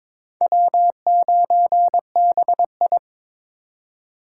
Morse code, heard as a keyed tone, about 23 wpm, 700 Hz. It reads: W9BI
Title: USA Call Signs - 4 Characters - 3x Character Spacing - 22wpm